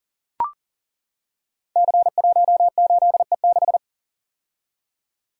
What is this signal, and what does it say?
Morse code, heard as a keyed tone, about 42 wpm, 700 Hz. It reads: C18E6
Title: Code Groups - 5 Letter:Numbers - 40wpm